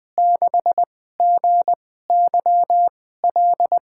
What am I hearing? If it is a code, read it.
Morse code, heard as a keyed tone, about 20 wpm, spaced as for 15 wpm, 700 Hz. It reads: 6GYL
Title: USA Call Signs - 2x Character Spacing - 20wpm